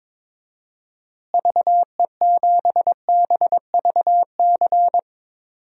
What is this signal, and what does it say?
Morse code, heard as a keyed tone, about 22 wpm, 700 Hz. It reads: VE7BVC
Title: DXCC-100 Call Signs - 6 Characters - Repeated 2x - 22wpm